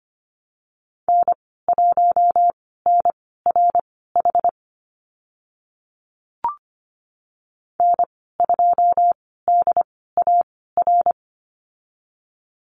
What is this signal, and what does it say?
Morse code, heard as a keyed tone, about 25 wpm, spaced as for 16 wpm, 700 Hz. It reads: N1NRH N2DAR
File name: Lesson 15g - Number 5 - Call Signs 25wpm